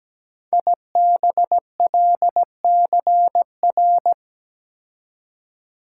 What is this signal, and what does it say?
Morse code, heard as a keyed tone, about 17 wpm, 700 Hz. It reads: IBLCR
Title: Code Groups - 5 Letter:Numbers - 17wpm